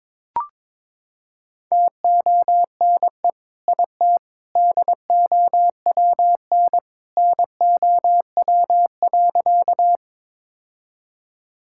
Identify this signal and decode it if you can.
Morse code, heard as a keyed tone, about 21 wpm, 700 Hz. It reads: TONE IT DOWN NOW.
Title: Sentences from Top 1600 Words 22wpm